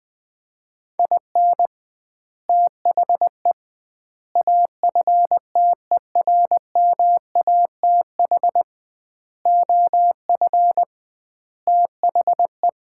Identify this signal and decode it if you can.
Morse code, heard as a keyed tone, about 20 wpm, 700 Hz. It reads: IN THE AFTERMATH OF THE
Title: 2x Word-Spacing - Sets of 5 Words - 20wpm